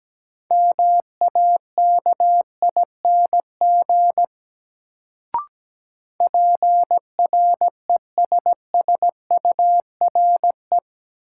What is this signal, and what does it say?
Morse code, heard as a keyed tone, about 17 wpm, 700 Hz. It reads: MAKING PRESSURE
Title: Sentences from Top 1000 Words - Review Part 6 - New Word Drill - 17wpm